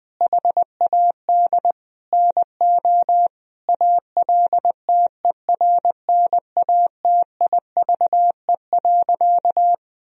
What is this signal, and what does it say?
Morse code, heard as a keyed tone, about 20 wpm, 700 Hz. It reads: HAD NO ALTERNATIVE.
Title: Sentences from Top 1000 Words - Review Part 4 - 20wpm